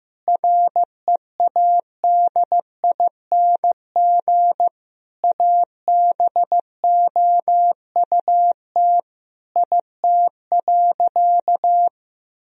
Morse code, heard as a keyed tone, about 15 wpm, 700 Hz. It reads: READING ABOUT IT.
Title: Sentences from Top 1000 Words - Review Part 3 - 15wpm